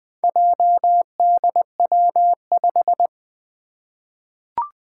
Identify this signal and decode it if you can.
Morse code, heard as a keyed tone, about 20 wpm, 700 Hz. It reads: JDW5
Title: Four Challenging Letter-Number Combinations 20wpm